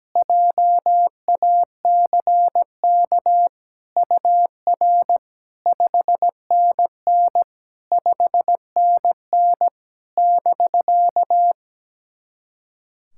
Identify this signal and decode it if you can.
Morse code, heard as a keyed tone, about 17 wpm, 700 Hz. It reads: JACK UR 5NN 5NN <BK>
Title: SOTA QSO Practice 17wpm